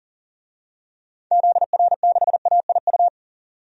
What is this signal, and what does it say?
Morse code, heard as a keyed tone, about 40 wpm, 700 Hz. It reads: ZR6AIU
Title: DXCC-100 Call Signs - 6 Characters - 40wpm